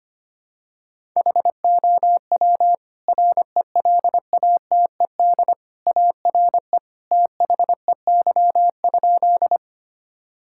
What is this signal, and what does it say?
Morse code, heard as a keyed tone, about 25 wpm, 700 Hz. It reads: HOW RELATED ARE THEY?
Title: Sentences from Top 1000 Words - Review Part 5 - 25wpm